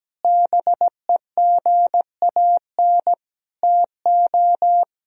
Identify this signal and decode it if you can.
Morse code, heard as a keyed tone, about 17 wpm, 700 Hz. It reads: BEGAN TO
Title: Sentences from Top 300 Words and Abbreviations 17wpm